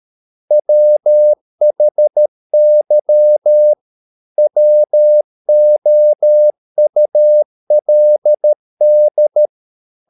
Morse code, heard as a keyed tone, about 13 wpm, 600 Hz. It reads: WHY WOULD